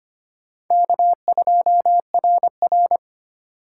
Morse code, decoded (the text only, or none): K2RR